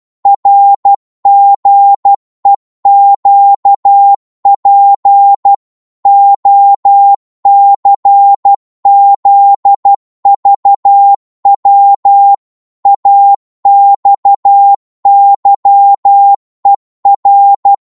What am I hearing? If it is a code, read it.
Morse code, heard as a keyed tone, about 12 wpm, 800 Hz. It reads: RGEQP OCZVW AXYER
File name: Morse / ltr-12wpm